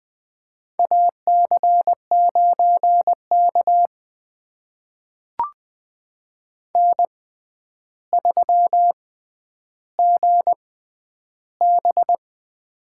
Morse code, heard as a keyed tone, about 21 wpm, spaced as for 10 wpm, 700 Hz. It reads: AC9K N3GB
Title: USA Call Signs - 4 Characters - 6x Character Spacing - 20wpm